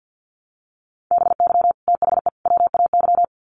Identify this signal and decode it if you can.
Morse code, heard as a keyed tone, about 53 wpm, 700 Hz. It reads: BY THE WAY
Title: Sets of 3 Words 50wpm